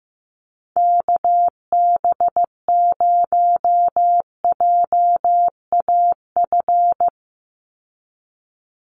Morse code, heard as a keyed tone, about 15 wpm, 700 Hz. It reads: KB0JAF
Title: USA Call Signs 15wpm